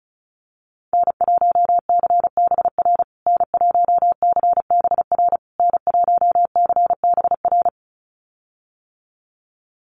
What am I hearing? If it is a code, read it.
Morse code, heard as a keyed tone, about 34 wpm, 700 Hz. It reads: N1CBR N1CBR N1CBR
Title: USA Call Signs Repeated 3x 35wpm